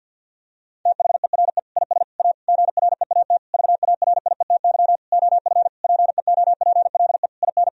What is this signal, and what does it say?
Morse code, heard as a keyed tone, about 50 wpm, 700 Hz. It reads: THERE IS A GREAT VARIETY OF PEOPLE IN
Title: Sentences from Top 1000 Words - Review Part 4 - 50wpm